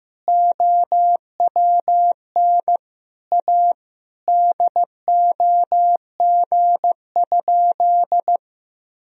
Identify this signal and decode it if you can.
Morse code, heard as a keyed tone, about 15 wpm, 700 Hz. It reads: OWN A DOG?
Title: Sentences from Top 400 Words and Abbreviations 15wpm